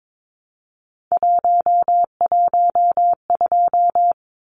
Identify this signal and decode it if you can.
Morse code, heard as a keyed tone, about 22 wpm, 700 Hz. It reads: 112
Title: Numbers - 3-Digit 22wpm